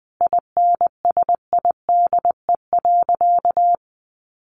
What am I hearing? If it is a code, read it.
Morse code, heard as a keyed tone, about 20 wpm, 700 Hz. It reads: INSIDE.
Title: Easy Phrases 20wpm